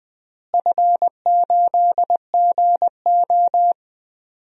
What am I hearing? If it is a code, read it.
Morse code, heard as a keyed tone, about 20 wpm, 690 Hz. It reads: F8GO
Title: DXCC-100 Call Signs - 4 Characters - Repeated 3x - 20wpm